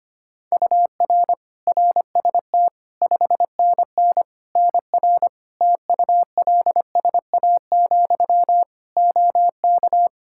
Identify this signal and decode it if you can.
Morse code, heard as a keyed tone, about 25 wpm, 700 Hz. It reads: UR RST 5NN NR TULSA, OK
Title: Practice QSO - Exchange 1 - Beginning - 25wpm